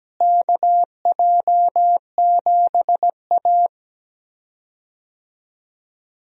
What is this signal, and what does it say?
Morse code, heard as a keyed tone, about 17 wpm, 700 Hz. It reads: KJ7A
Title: USA Call Signs - 4 Characters - Repeated 3x - 17wpm